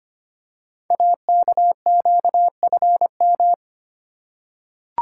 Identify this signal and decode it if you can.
Morse code, heard as a keyed tone, about 25 wpm, 700 Hz. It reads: AKQFM
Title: Code Groups - 5 Letter:Numbers - 25wpm